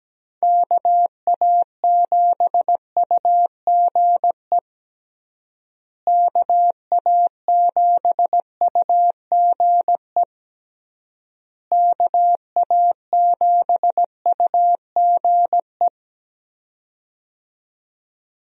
Morse code, heard as a keyed tone, about 17 wpm, 700 Hz. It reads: KA7UGE KA7UGE KA7UGE
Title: USA Call Signs - 6 Characters - Repeated 3x - 17wpm